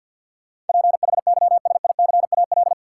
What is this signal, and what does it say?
Morse code, heard as a keyed tone, about 53 wpm, 700 Hz. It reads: PHYSICAL
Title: Sentences from Top 1000 Words - Review Part 4 - New Word Drill - 50wpm